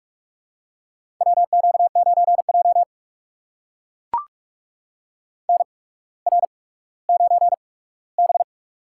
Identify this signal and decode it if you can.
Morse code, heard as a keyed tone, about 44 wpm, spaced as for 19 wpm, 700 Hz. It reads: WQ9J NR9B
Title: USA Call Signs - 4 Characters - 8x Character Spacing - 45wpm